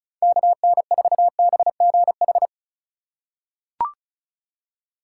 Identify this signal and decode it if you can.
Morse code, heard as a keyed tone, about 36 wpm, 700 Hz. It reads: KN4BGH
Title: USA Call Signs - 6 Characters - 4x Character Spacing - 35wpm